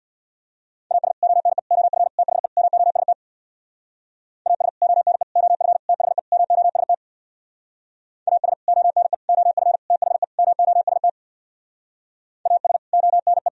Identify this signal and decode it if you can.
Morse code, heard as a keyed tone, about 64 wpm, 700 Hz. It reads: AS ONE OF THE MOST AS ONE OF THE MOST AS ONE OF THE MOST AS ONE